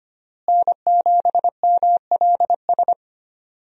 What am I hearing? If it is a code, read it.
Morse code, heard as a keyed tone, about 25 wpm, 700 Hz. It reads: N7MLS
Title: USA Call Signs - 5 Characters - Repeated 2x - 25wpm